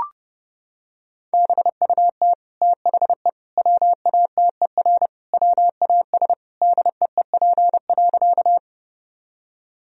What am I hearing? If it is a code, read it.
Morse code, heard as a keyed tone, about 30 wpm, 700 Hz. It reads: BUT THE WATER WAS DEEP.
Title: Sentences from Top 1000 Words - Review Part 3 - 30wpm